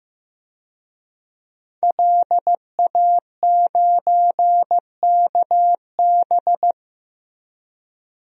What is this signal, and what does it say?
Morse code, heard as a keyed tone, about 16 wpm, 700 Hz. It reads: LA9KB
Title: DXCC-100 Call Signs - 5 Characters - Repeated 3x - 15wpm